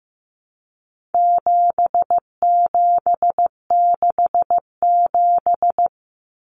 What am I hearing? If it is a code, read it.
Morse code, heard as a keyed tone, about 15 wpm, 700 Hz. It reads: 7767
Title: Numbers - 4-Digit 15wpm